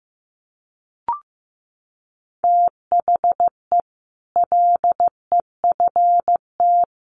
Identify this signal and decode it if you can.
Morse code, heard as a keyed tone, about 15 wpm, 700 Hz. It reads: THE LEFT
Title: Sets of 5 Words Encore 15wpm